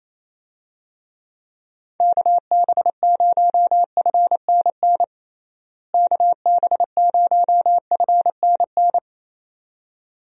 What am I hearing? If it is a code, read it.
Morse code, heard as a keyed tone, about 28 wpm, 700 Hz. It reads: KB0FNN KB0FNN
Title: USA Call Signs - 6 Characters - Repeated 2x - 28wpm